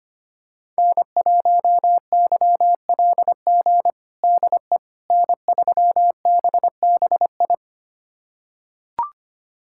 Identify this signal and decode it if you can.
Morse code, heard as a keyed tone, about 25 wpm, 700 Hz. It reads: N1YLG DE N3BBI
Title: USA Callsign DE Callsign - 5 Characters - 25wpm